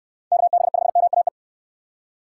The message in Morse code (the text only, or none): KB5MAE